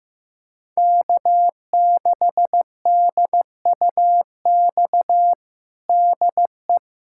Morse code, heard as a keyed tone, about 15 wpm, 700 Hz. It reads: K6DUX DE
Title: USA Callsign DE Callsign - 5 Characters - 15wpm